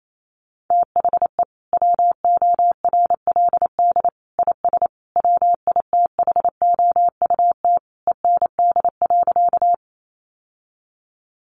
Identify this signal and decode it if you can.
Morse code, heard as a keyed tone, about 28 wpm, 700 Hz. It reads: THE WORLD IS WITHOUT END.
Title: Sentences from Top 600 Words 28wpm